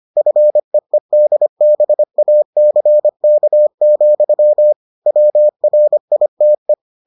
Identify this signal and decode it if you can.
Morse code, heard as a keyed tone, about 25 wpm, 600 Hz. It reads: FEEDBACK, WRITE